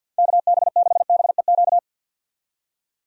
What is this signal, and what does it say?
Morse code, heard as a keyed tone, about 48 wpm, 700 Hz. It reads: KB6BEQ
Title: USA Call Signs - 6 Characters - 50wpm